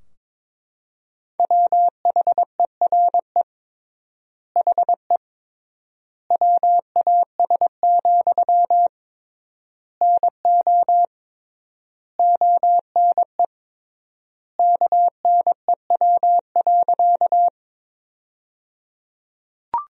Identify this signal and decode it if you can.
Morse code, heard as a keyed tone, about 21 wpm, 700 Hz. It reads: WHERE HE WAS, NO ONE KNEW.
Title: 3x Word-Spacing - Sentences from Top 400 Words - 22wpm